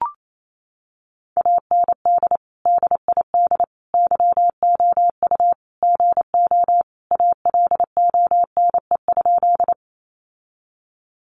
Morse code, heard as a keyed tone, about 28 wpm, 700 Hz. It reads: AND DID YOU GO ALONE?
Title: Sentences from Top 1000 Words 28wpm